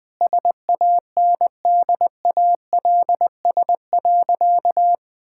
Morse code, heard as a keyed tone, about 21 wpm, 700 Hz. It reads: SANDALS.
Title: All-Age Appropriate Jokes - 20wpm